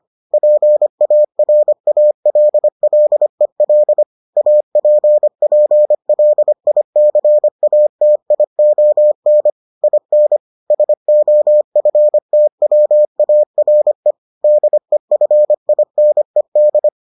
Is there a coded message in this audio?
Yes: Morse code from 0.3 to 16.9 s, heard as a keyed tone, about 25 wpm, 600 Hz. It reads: PARALLEL APPLICATION IN SOFTWARE DEFINED